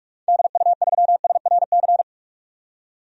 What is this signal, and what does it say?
Morse code, heard as a keyed tone, about 42 wpm, 700 Hz. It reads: DU3SRC
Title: DXCC-100 Call Signs - 6 Characters - 45wpm